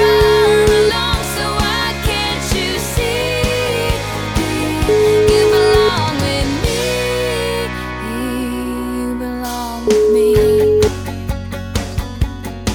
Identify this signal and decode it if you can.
Ringback tone (Europe, 425 Hz), with noise.